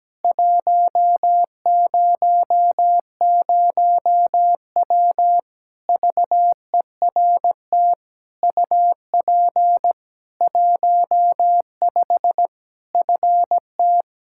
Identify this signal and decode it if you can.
Morse code, heard as a keyed tone, about 17 wpm, 700 Hz. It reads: 100W VERT UP 15 FT
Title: K1USN SST QSO Practice 17wpm